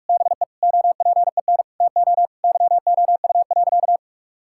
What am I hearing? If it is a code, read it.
Morse code, heard as a keyed tone, about 45 wpm, 700 Hz. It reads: BE OPEN TO YOU.
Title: Sentences from Top 1600 Words 45wpm